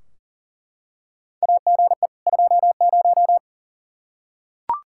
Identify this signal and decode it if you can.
Morse code, heard as a keyed tone, about 40 wpm, 700 Hz. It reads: AGE 20
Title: Practice QSO - Exchange 3 - More Info - 40wpm